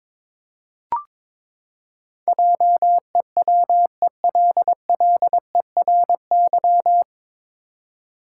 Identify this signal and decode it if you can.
Morse code, heard as a keyed tone, about 22 wpm, 700 Hz. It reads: JEWELLERY
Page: Morse Code Ninja practice 22wpm